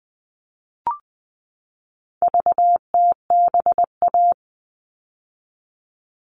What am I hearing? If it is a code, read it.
Morse code, heard as a keyed tone, about 20 wpm, 700 Hz. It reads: VTBA
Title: All Cut Numbers - 4-Digit 20wpm